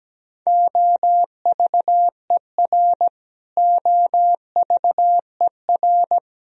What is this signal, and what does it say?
Morse code, heard as a keyed tone, about 17 wpm, 700 Hz. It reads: OVER OVER
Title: Top 100 Words Repeated 17wpm